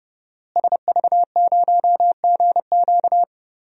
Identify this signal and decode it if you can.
Morse code, heard as a keyed tone, about 31 wpm, 700 Hz. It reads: SV0GQ
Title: DXCC-100 Call Signs - 5 Characters - Repeated 3x - 30wpm